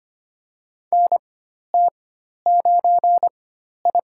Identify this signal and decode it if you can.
Morse code, heard as a keyed tone, about 25 wpm, spaced as for 12 wpm, 700 Hz. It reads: NT9I
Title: USA Call Signs - 4 Characters - 4x Character Spacing - 25wpm